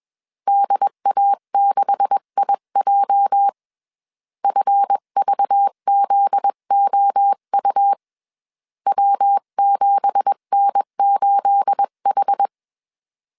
Morse code, heard as a keyed tone, about 21 wpm, 800 Hz. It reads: DA6IJ FVZOU W7N8H